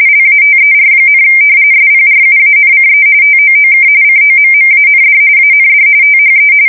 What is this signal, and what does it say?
RTTY, about 45.45 baud. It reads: LATER, SPRUANCE TOOK COMMAND AND THE
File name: fsk170 / battleshipnj